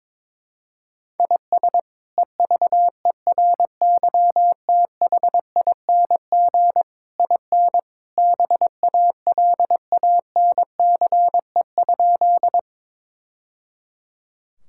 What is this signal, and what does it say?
Morse code, heard as a keyed tone, about 22 wpm, 700 Hz. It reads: IS EVERYTHING IN BALANCE?
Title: Sentences from Top 1000 Words - Review Part 6 - 22wpm